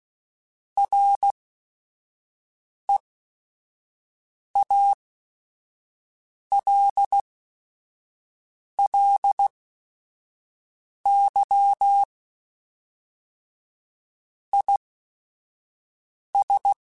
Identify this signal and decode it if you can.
Morse code, heard as a keyed tone, about 16 wpm, spaced as for 5 wpm, 800 Hz. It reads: REALLY IS